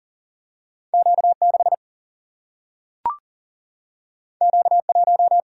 Morse code, heard as a keyed tone, about 40 wpm, 700 Hz. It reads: Q6 Q1
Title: Single-Letter Single-Number 40wpm